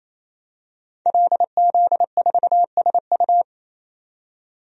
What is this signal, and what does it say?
Morse code, heard as a keyed tone, about 28 wpm, 700 Hz. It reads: LZ4SU
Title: DXCC-100 Call Signs - 5 Characters - 28wpm